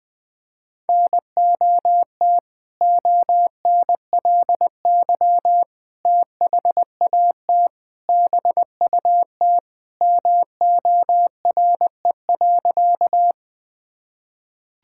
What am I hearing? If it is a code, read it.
Morse code, heard as a keyed tone, about 20 wpm, 700 Hz. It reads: NOT ONLY THAT BUT MORE.
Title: Sentences with 2-Word Phrases - Spoken Beginning - 20wpm